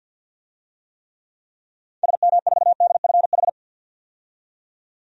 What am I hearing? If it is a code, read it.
Morse code, heard as a keyed tone, about 52 wpm, 700 Hz. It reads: SM4DFH
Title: DXCC-100 Call Signs - 6 Characters - 50wpm